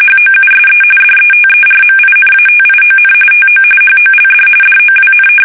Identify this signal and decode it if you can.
RTTY, about 45.45 baud. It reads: HOUSE DETERMINATION TO HOLD DOWN
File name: fsk850 / usnavy5